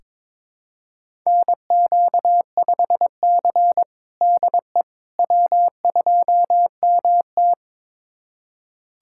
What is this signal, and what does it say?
Morse code, heard as a keyed tone, about 22 wpm, 700 Hz. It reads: NQ5C DE W2MT
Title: USA Callsign DE Callsign - 4 Characters - 22wpm